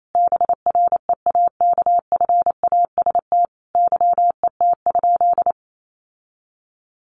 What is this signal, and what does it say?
Morse code, heard as a keyed tone, about 28 wpm, 700 Hz. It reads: BREAKFAST YET?
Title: Sentences from Top 1500 Words 28wpm